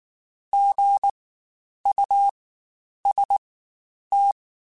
Morse code, heard as a keyed tone, about 19 wpm, spaced as for 9 wpm, 800 Hz. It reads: GUST
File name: Spaced Alphas / Spaced Alphas mp3s / Gs sp @18